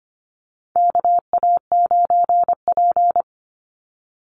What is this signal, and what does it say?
Morse code, heard as a keyed tone, about 25 wpm, 700 Hz. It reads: KA9P
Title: Call Signs Repeated 2x - 25wpm